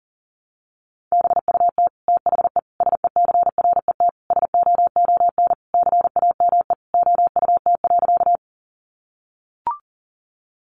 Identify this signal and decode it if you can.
Morse code, heard as a keyed tone, about 40 wpm, 700 Hz. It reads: BUT THE SECRET SOON CAME OUT.
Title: Sentences from Top 1200 Words 40wpm